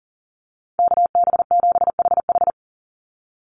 Morse code, heard as a keyed tone, about 40 wpm, 700 Hz. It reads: KB7HH
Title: NA SOTA Chaser Callsigns 40wpm